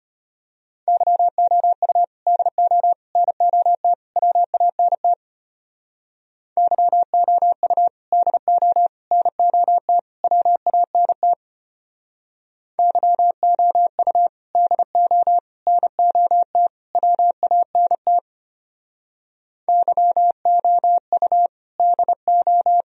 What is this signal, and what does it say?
Morse code, heard as a keyed tone, about 31 wpm, 700 Hz. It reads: YOU DO NOT WANT YOU DO NOT WANT YOU DO NOT WANT YOU DO
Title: Speed-Racing - Sets of 4 Words 25wpm